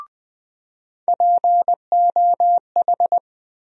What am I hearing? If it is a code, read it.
Morse code, heard as a keyed tone, about 20 wpm, 700 Hz. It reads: POH
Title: USA Call Sign Postfixes - 3 Letters 20wpm